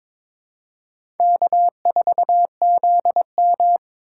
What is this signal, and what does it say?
Morse code, heard as a keyed tone, about 22 wpm, 700 Hz. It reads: K4ZM
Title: USA Call Signs - 3x Character Spacing - 22wpm